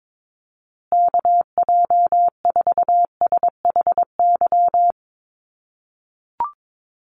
Morse code, heard as a keyed tone, about 22 wpm, 700 Hz. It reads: KJ4SHY
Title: USA Call Signs Repeated 3x 22wpm